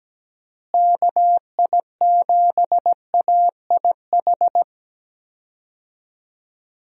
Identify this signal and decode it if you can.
Morse code, heard as a keyed tone, about 17 wpm, 700 Hz. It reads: KI7AIH
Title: USA Call Signs - 6 Characters - Repeated 3x - 17wpm